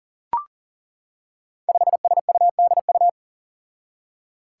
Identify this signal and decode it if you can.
Morse code, heard as a keyed tone, about 40 wpm, 700 Hz. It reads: 5SUDU
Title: Five Challenging Letter-Number Combinations 40wpm